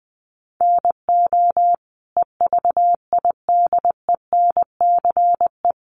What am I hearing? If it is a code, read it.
Morse code, heard as a keyed tone, about 20 wpm, 700 Hz. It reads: NO EVIDENCE
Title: Sentences from Top 1300 Words 20wpm